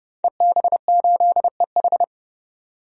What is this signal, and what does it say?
Morse code, heard as a keyed tone, about 30 wpm, 700 Hz. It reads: EB8EH
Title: DXCC-100 Call Signs - 5 Characters - 30wpm